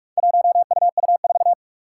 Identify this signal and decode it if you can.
Morse code, heard as a keyed tone, about 44 wpm, 700 Hz. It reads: 1UU4
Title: Four Challenging Letter-Number Combinations 45wpm